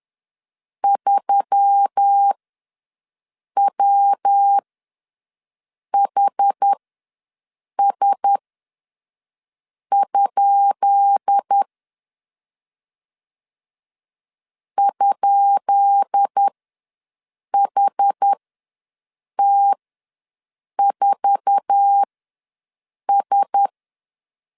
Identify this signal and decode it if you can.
Morse code, heard as a keyed tone, about 10 wpm, spaced as for 5 wpm, 800 Hz. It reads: 3WHS? ?HT4S
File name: LEZIONI / 10 SIGNS / 10B INT ws 10wpm 4134